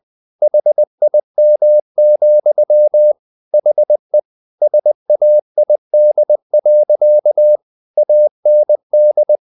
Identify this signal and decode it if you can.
Morse code, heard as a keyed tone, about 20 wpm, 600 Hz. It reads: HIM, HE SAID. AND
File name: the old man and the sea / the old man and the sea part 1 20wpm